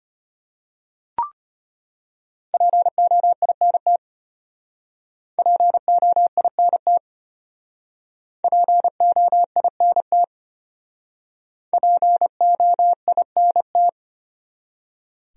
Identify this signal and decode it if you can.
Morse code, heard as a keyed tone, about 33 wpm, 700 Hz. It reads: POINT POINT POINT POINT